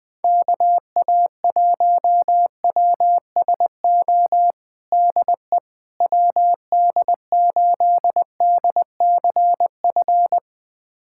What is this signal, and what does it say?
Morse code, heard as a keyed tone, about 20 wpm, 700 Hz. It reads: KA1WSO DE WD8DCF
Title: USA Callsign DE Callsign - 6 Characters - 20wpm